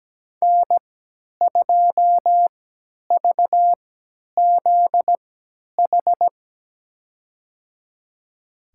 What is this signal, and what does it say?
Morse code, heard as a keyed tone, about 17 wpm, spaced as for 10 wpm, 700 Hz. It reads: N2VZH
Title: USA Call Signs - 3x Character Spacing - 17wpm